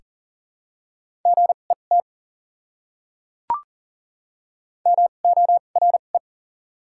Morse code, heard as a keyed tone, about 40 wpm, spaced as for 29 wpm, 700 Hz. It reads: GET MORE
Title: 2x Character Spacing - Top 200 Words - 40wpm